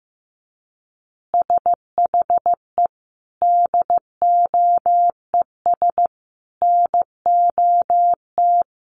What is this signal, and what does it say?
Morse code, heard as a keyed tone, about 15 wpm, 700 Hz. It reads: SHE DOES NOT
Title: Additional Easy Phrases 15wpm